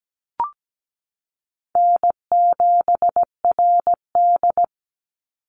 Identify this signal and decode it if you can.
Morse code, heard as a keyed tone, about 17 wpm, 700 Hz. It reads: N7RD
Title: Easy Call Signs Repeated 3x - 17wpm